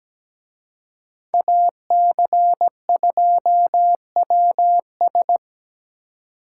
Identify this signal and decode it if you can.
Morse code, heard as a keyed tone, about 20 wpm, 700 Hz. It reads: AC2WS
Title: Speed Racing - USA 2x2 Call Signs 17wpm